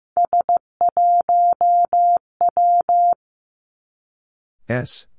Morse code, heard as a keyed tone, about 15 wpm, 700 Hz. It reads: S1W
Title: Three Challenging Letter-Number Combinations 15wpm